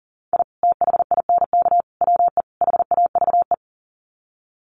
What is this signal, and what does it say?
Morse code, heard as a keyed tone, about 40 wpm, 700 Hz. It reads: I THINK WE HAVE
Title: Sets of 4 Words 40wpm